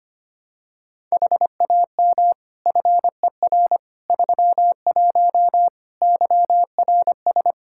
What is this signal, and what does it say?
Morse code, heard as a keyed tone, about 25 wpm, 700 Hz. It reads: HAM FER 31 YRS
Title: Practice QSO - Exchange 3 - More Info - 25wpm